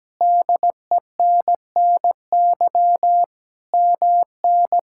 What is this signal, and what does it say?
Morse code, heard as a keyed tone, about 17 wpm, 700 Hz. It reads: DENNY MN
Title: K1USN SST QSO Practice - 17wpm